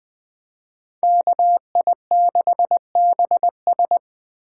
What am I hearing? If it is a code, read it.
Morse code, heard as a keyed tone, about 20 wpm, 700 Hz. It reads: KI6BS
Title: Speed Racing - USA 2x2 Call Signs 17wpm